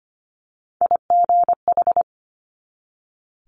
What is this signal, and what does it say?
Morse code, heard as a keyed tone, about 25 wpm, 700 Hz. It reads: IGH
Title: Multi-letter Phonemes 25wpm